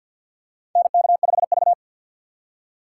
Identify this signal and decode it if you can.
Morse code, heard as a keyed tone, about 52 wpm, 700 Hz. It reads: NK5V
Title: USA Call Signs - 3x Character Spacing - 50wpm